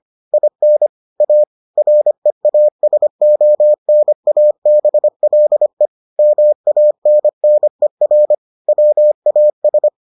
Morse code, heard as a keyed tone, about 25 wpm, 600 Hz. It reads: IN A REASONABLE MANNER WAS